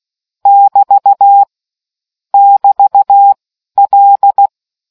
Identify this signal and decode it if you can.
Morse code, heard as a keyed tone, about 16 wpm, 800 Hz. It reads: <BT> <BT> L